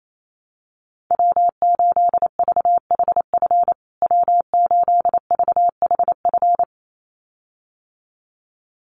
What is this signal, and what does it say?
Morse code, heard as a keyed tone, about 28 wpm, 700 Hz. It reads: W8VHF W8VHF
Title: USA Call Signs Repeated 2x 28wpm